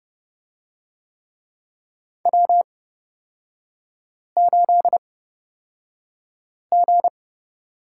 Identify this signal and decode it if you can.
Morse code, heard as a keyed tone, about 30 wpm, spaced as for 5 wpm, 700 Hz. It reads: W8G
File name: Lesson 37e - Letter x - Three Characters 30wpm